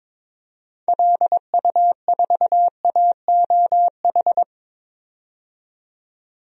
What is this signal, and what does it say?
Morse code, heard as a keyed tone, about 22 wpm, 700 Hz. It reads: LU4AOH
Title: DXCC-100 Call Signs - 6 Characters - 22wpm